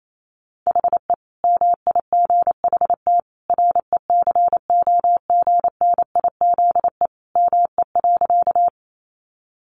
Morse code, heard as a keyed tone, about 28 wpm, 700 Hz. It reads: HE MIGHT RECOGNIZE ME.
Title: Sentences from Top 1400 Words 28wpm